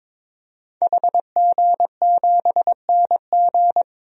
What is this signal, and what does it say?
Morse code, heard as a keyed tone, about 21 wpm, 700 Hz. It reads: HG7NG
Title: DXCC-100 Call Signs - 5 Characters - Repeated 2x - 22wpm